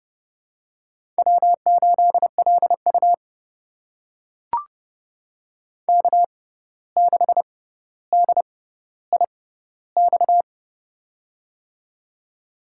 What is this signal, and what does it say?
Morse code, heard as a keyed tone, about 30 wpm, spaced as for 14 wpm, 700 Hz. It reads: W8LU K6DIX
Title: USA Call Signs - 6x Character Spacing - 30wpm